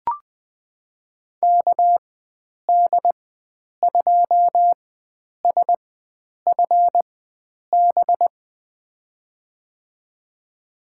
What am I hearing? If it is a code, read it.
Morse code, heard as a keyed tone, about 20 wpm, spaced as for 9 wpm, 700 Hz. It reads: KD2SFB
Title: USA Call Signs - 6 Characters - 4x Character Spacing - 20wpm